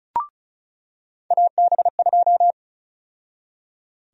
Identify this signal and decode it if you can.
Morse code, heard as a keyed tone, about 36 wpm, 700 Hz. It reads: AB2
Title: USA Call Sign Prefixes - 35wpm